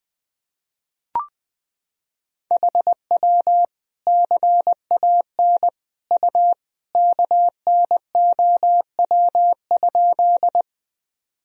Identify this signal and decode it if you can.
Morse code, heard as a keyed tone, about 21 wpm, 700 Hz. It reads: HW CAN U KNOW?
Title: Sentences from Top 100 Words and Abbreviations 20wpm